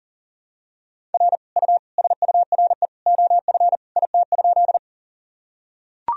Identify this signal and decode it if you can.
Morse code, heard as a keyed tone, about 40 wpm, 700 Hz. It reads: R U SURE OF IT?
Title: Sentences from Top 300 Words and Abbreviations 40wpm